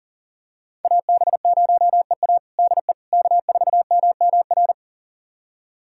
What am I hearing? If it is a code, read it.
Morse code, heard as a keyed tone, about 40 wpm, 700 Hz. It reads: AB0EA DE K4MMR
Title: USA Callsign DE Callsign - 5 Characters - 40wpm